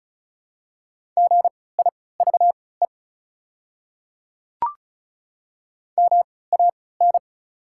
Morse code, heard as a keyed tone, about 34 wpm, spaced as for 20 wpm, 700 Hz. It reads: GIVE MAN
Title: 3x Character Spacing - Top 200 Words - 35wpm